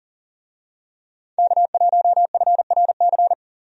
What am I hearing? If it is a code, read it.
Morse code, heard as a keyed tone, about 37 wpm, 700 Hz. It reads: K1FRC
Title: USA Call Signs - 5 Characters - Repeated 2x - 40wpm